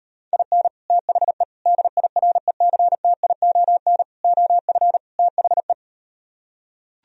Morse code, heard as a keyed tone, about 39 wpm, 700 Hz. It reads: IN THE DIRECTION OF THE